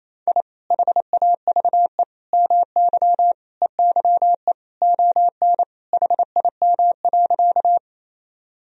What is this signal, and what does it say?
Morse code, heard as a keyed tone, about 28 wpm, 700 Hz. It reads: I HAVE MY EYE ON HIM.
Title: Sentences from Top 1000 Words - Review Part 1 - 28wpm